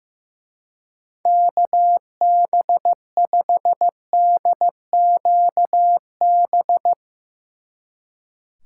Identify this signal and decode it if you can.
Morse code, heard as a keyed tone, about 16 wpm, 700 Hz. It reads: KB5DQB
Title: Speed Racing - USA Call Signs 15wpm